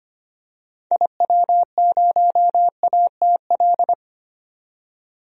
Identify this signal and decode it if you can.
Morse code, heard as a keyed tone, about 25 wpm, 700 Hz. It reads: IW0ATL
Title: DXCC-100 Call Signs - 6 Characters - 25wpm